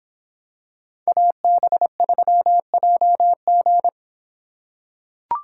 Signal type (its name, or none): Morse code